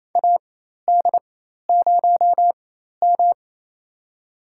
Morse code, heard as a keyed tone, about 28 wpm, spaced as for 13 wpm, 700 Hz. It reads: AD0M
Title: USA Call Signs - 4 Characters - 4x Character Spacing - 28wpm